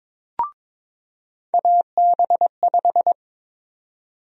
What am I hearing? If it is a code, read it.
Morse code, heard as a keyed tone, about 22 wpm, 700 Hz. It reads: AB5
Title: USA Call Sign Prefixes - 22wpm